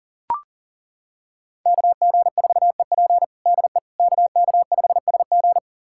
Morse code, heard as a keyed tone, about 40 wpm, 700 Hz. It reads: KG4EP DE KK5SG
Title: USA Callsign DE Callsign - 5 Characters - 40wpm